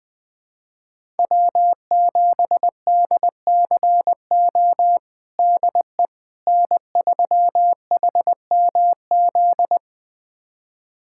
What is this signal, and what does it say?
Morse code, heard as a keyed tone, about 20 wpm, 700 Hz. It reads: W7DCO DE N3HMZ
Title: USA Callsign DE Callsign - 5 Characters - 20wpm